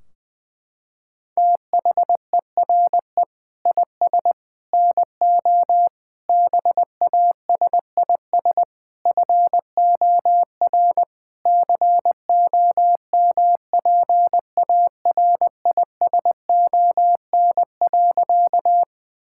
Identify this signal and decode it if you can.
Morse code, heard as a keyed tone, about 20 wpm, 700 Hz. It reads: THERE IS NO BASIS FOR COMPARISON.
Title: Sentences from Top 1600 Words 20wpm